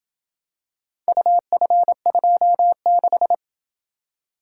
Morse code, heard as a keyed tone, about 27 wpm, 700 Hz. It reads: UF26